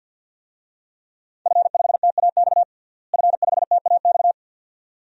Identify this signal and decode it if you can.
Morse code, heard as a keyed tone, about 46 wpm, 700 Hz. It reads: F5TAX F5TAX
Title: DXCC-100 Call Signs - 5 Characters - Repeated 2x - 50wpm